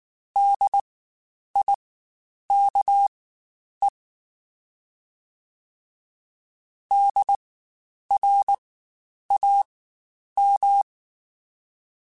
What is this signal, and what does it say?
Morse code, heard as a keyed tone, about 19 wpm, spaced as for 9 wpm, 800 Hz. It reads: DIKE DRAM
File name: Spaced Alphas / Spaced Alphas mp3s / Ds sp @18